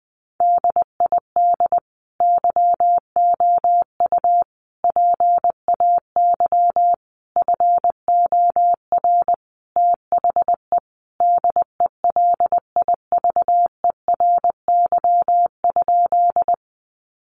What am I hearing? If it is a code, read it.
Morse code, heard as a keyed tone, about 20 wpm, 700 Hz. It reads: DID YOU PAY FOR THE DELIVERY?
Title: Sentences from Top 1500 Words 20wpm